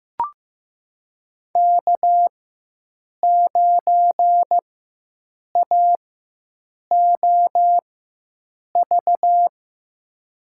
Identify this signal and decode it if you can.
Morse code, heard as a keyed tone, about 15 wpm, spaced as for 7 wpm, 700 Hz. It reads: K9AOV
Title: USA Call Signs - 4x Character Spacing - 15wpm